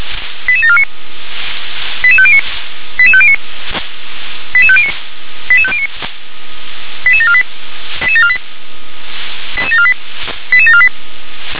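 Five-tone selective call (ZVEI), heard as quick runs of five tones, tones of about 70 ms. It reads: 88749 88509 88509 88509 88509 88748 88748 88748 88748
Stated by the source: ZVEI five-tone selective call received by SDR.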